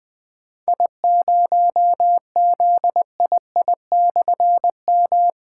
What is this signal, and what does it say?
Morse code, heard as a keyed tone, about 20 wpm, 700 Hz. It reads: I0ZII/M
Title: Difficult Call Signs 20wpm